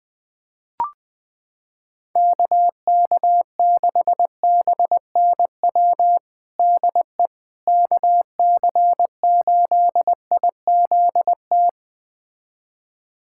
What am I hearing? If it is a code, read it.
Morse code, heard as a keyed tone, about 21 wpm, 700 Hz. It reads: KK6BNW DE KC8IZT
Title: USA Callsign DE Callsign 20wpm